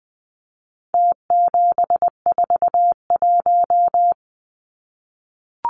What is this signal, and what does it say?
Morse code, heard as a keyed tone, about 20 wpm, 700 Hz. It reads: T741
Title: Numbers - Contest Style - 4-Digit 20wpm